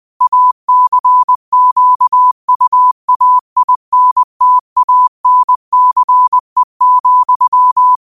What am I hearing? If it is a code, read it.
Morse code, heard as a keyed tone, about 20 wpm, 1000 Hz. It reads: ACQUAINTANCE,
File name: qotdfiles / 20wpm